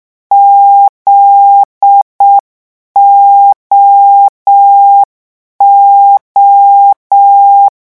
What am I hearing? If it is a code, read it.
Morse code, heard as a keyed tone, about 6 wpm, 800 Hz. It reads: ZOO
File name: morse mp3 / lecon9-vitesse 6